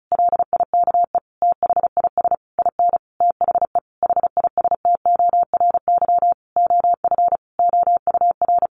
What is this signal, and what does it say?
Morse code, heard as a keyed tone, about 34 wpm, 700 Hz. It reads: LIKE THIS IN THE HISTORY OF OUR
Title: Sentences from Top 1500 Words 35wpm